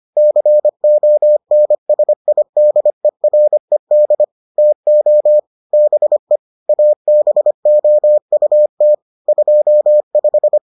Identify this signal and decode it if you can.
Morse code, heard as a keyed tone, about 25 wpm, 600 Hz. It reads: CONSIDERED TO BE ABOUT 25